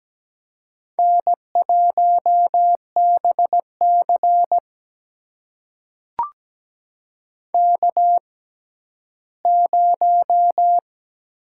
Morse code, heard as a keyed tone, about 17 wpm, spaced as for 12 wpm, 700 Hz. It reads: N1BC K0
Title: USA Call Signs - 4 Characters - 6x Character Spacing - 17wpm